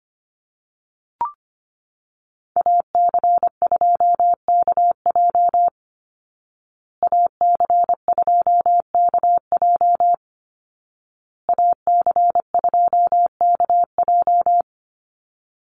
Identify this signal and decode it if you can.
Morse code, heard as a keyed tone, about 25 wpm, 700 Hz. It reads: AC2KJ AC2KJ AC2KJ
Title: Call Signs Repeated 2x - 25wpm